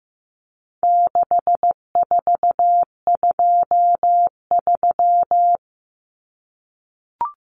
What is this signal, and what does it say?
Morse code, heard as a keyed tone, about 15 wpm, 700 Hz. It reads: 6423
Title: Numbers - 4-Digit 15wpm